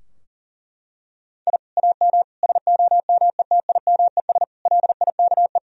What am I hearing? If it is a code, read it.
Morse code, heard as a keyed tone, about 40 wpm, 700 Hz. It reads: I AM SOMETIMES LIKE
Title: Sentences from Top 1000 Words - Review Part 3 - 40wpm